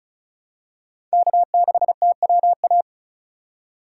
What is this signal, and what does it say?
Morse code, heard as a keyed tone, about 36 wpm, 700 Hz. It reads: K6TWA
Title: USA Call Signs - 5 Characters - 4x Character Spacing - 35wpm